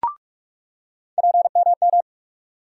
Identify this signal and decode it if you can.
Morse code, heard as a keyed tone, about 46 wpm, 700 Hz. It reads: PMM
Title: USA Call Sign Postfixes - 3 Letters 45wpm